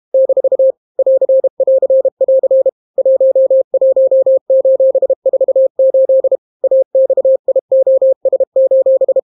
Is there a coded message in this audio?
Yes: Morse code from 0.1 to 9.2 s, heard as a keyed tone, about 32 wpm, 530 Hz. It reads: - <AR><AR><AR> 11:48 AXIOS: